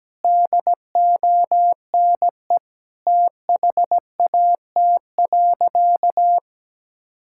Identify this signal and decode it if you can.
Morse code, heard as a keyed tone, about 17 wpm, 700 Hz. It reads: DONE THAT.
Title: Sentences from Top 1600 Words 17wpm